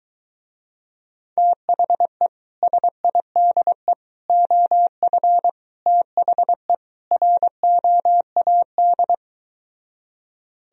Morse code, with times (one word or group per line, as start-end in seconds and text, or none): 1.4-2.3 THE
2.6-3.9 SIDE
4.3-5.5 OF
5.9-6.7 THE
7.1-9.1 ROAD